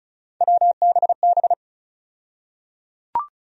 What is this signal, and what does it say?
Morse code, heard as a keyed tone, about 34 wpm, 700 Hz. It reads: WBB
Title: USA Call Sign Postfixes - 3 Letters 35wpm